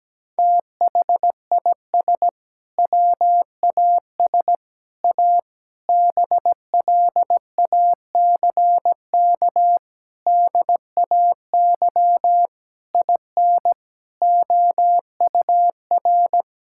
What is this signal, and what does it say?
Morse code, heard as a keyed tone, about 17 wpm, 700 Hz. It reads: THIS WAS A BLACK DAY IN OUR
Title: Sentences from Top 1600 Words 17wpm